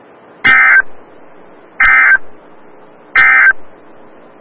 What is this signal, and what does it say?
Emergency Alert System (SAME) tones, heard as harsh buzzing bursts. It is heard from 0.3 to 3.6 s.
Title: Des Moines, IA NOAA Weather Radio - WXL57